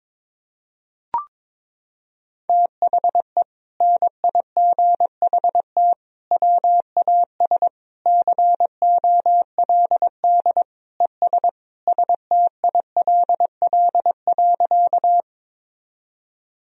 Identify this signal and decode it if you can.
Morse code, heard as a keyed tone, about 22 wpm, 700 Hz. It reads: THE NIGHT WAS COLD ES STILL.
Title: Sentences from Top 400 Words and Abbreviations 22wpm